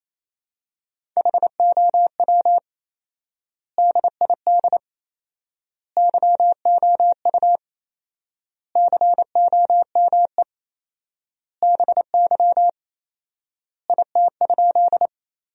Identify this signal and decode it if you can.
Morse code, heard as a keyed tone, about 27 wpm, 700 Hz. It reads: HOW DID YOU COME BY IT?
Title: 4x Word-Spacing - Sentences from Top 100 Words - 28wpm